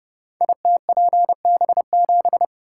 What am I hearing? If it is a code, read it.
Morse code, heard as a keyed tone, about 30 wpm, 700 Hz. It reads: ITPB7
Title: Code Groups - 5 Letter:Numbers - 30wpm